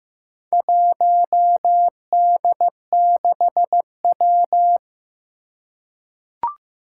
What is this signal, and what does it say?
Morse code, heard as a keyed tone, about 15 wpm, 700 Hz. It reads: 1D6W
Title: Four Challenging Letter-Number Combinations 15wpm